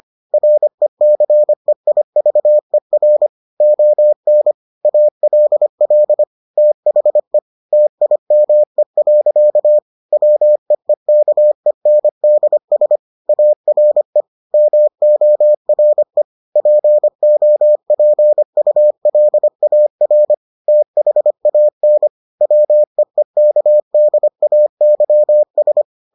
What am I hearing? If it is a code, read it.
Morse code, heard as a keyed tone, about 25 wpm, 600 Hz. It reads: RECEIVER ON ALL THE TIME. WEEKENDS ARE MORE POPULAR THAN WEEKDAYS